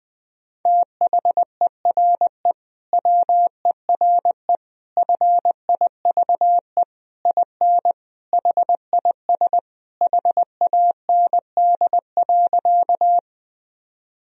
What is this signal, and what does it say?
Morse code, heard as a keyed tone, about 20 wpm, 700 Hz. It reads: THERE WERE FIVE IN HIS HAND.
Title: Sentences from Top 1000 Words - Review Part 2 - 20wpm